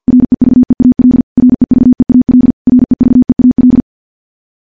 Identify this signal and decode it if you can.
Morse code, heard as a keyed tone, about 50 wpm, 250 Hz. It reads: REVEAL REVEAL REVEAL
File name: morse-training-50WPM